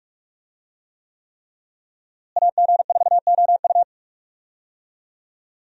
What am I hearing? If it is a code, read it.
Morse code, heard as a keyed tone, about 46 wpm, 700 Hz. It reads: AG4OU
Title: USA Call Signs - 5 Characters - Repeated 3x - 45wpm